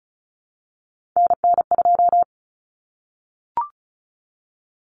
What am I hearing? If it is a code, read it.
Morse code, heard as a keyed tone, about 34 wpm, 700 Hz. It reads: NN2
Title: Common Cut Numbers - 3-Digit 35wpm